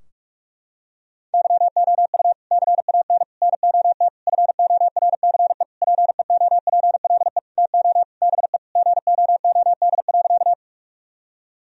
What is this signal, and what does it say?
Morse code, heard as a keyed tone, about 46 wpm, 700 Hz. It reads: YOU CAN NOT FORCE PEOPLE TO BE GOOD.
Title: Sentences from Top 1000 Words - Review Part 3 - 45wpm